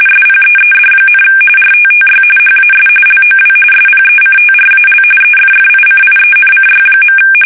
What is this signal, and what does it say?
RTTY, about 45.45 baud. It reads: SPEED, RTTY HAS LOW SPECTRAL EFFICIENCY.
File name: fsk850 / teletype-RTTY